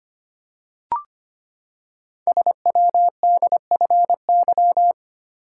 Morse code, heard as a keyed tone, about 25 wpm, 700 Hz. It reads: SWDFY
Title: Code Groups - 5 Letters - 25wpm